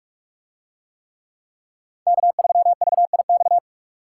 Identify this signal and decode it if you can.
Morse code, heard as a keyed tone, about 45 wpm, 700 Hz. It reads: K3VIX